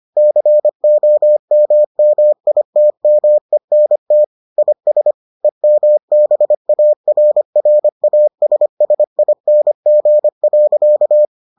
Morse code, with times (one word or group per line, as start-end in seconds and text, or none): 0.2-4.2 COMMITMENT
4.6-5.1 IS
5.4-11.3 EMBARRASSING.